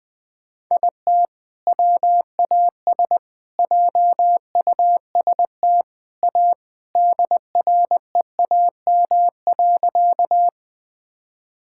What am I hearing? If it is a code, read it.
Morse code, heard as a keyed tone, about 21 wpm, 700 Hz. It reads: IT WAS JUST A DREAM.
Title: Sentences with 3-Word Phrases - 20wpm